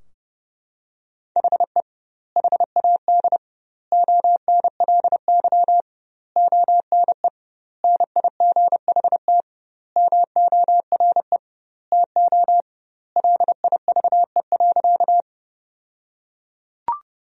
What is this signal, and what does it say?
Morse code, heard as a keyed tone, about 30 wpm, 700 Hz. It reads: HE HAD ONLY ONE NIGHT MORE TO LIVE.
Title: 2x Word-Spacing - Sentences from Top 300 Words - 30wpm